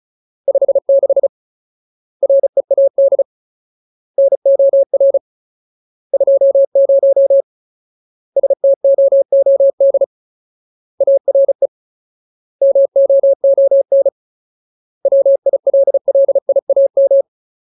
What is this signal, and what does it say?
Morse code, heard as a keyed tone, about 34 wpm, 550 Hz. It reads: 56 READ NOR 20 STOOD ARE MOON WILLIAM
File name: Ultimate Word List / 35WPM Ultimate